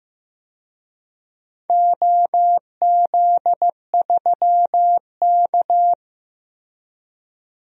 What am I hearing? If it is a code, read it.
Morse code, heard as a keyed tone, about 16 wpm, 700 Hz. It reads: OZ3K
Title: DXCC-100 Call Signs - 4 Characters - Repeated 3x - 15wpm